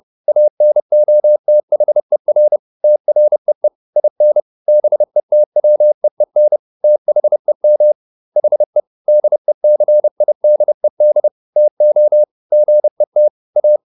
Morse code, heard as a keyed tone, about 30 wpm, 600 Hz. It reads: ANOTHER TREE IN BETWEEN THEM HE DECIDED TO GET A
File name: SS302 30